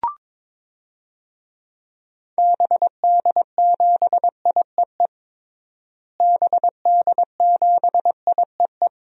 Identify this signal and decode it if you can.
Morse code, heard as a keyed tone, about 22 wpm, 700 Hz. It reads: BD7IEE BD7IEE
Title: DXCC-100 Call Signs - 6 Characters - Repeated 2x - 22wpm